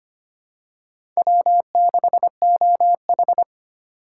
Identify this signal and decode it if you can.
Morse code, heard as a keyed tone, about 25 wpm, 700 Hz. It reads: W6OH
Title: USA Call Signs - 2x Character Spacing - 25wpm